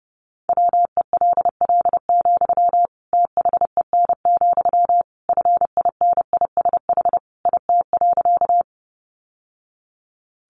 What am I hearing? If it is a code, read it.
Morse code, heard as a keyed tone, about 30 wpm, 700 Hz. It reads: WELL, THEN, FINISH IT.
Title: Sentences from Top 1200 Words 30wpm